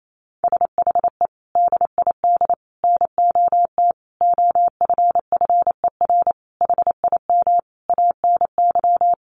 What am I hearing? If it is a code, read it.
Morse code, heard as a keyed tone, about 28 wpm, 700 Hz. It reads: SHE DID NOT OFFER HIM ANY
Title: Sentences from Top 1500 Words 28wpm